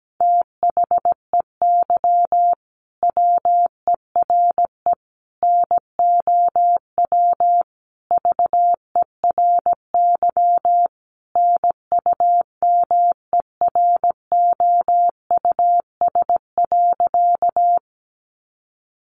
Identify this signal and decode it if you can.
Morse code, heard as a keyed tone, about 17 wpm, 700 Hz. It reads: THEY WERE NOW VERY NUMEROUS.
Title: Sentences from Top 1400 Words 17wpm